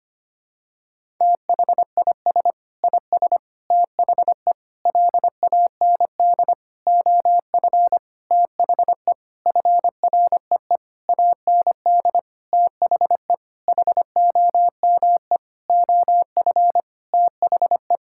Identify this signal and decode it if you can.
Morse code, heard as a keyed tone, about 25 wpm, 700 Hz. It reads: THIS IS THE LAND OF THE FREE AND THE HOME OF THE
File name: practice / 25wpm / Sentences with 9-Word Phrases - 25wpm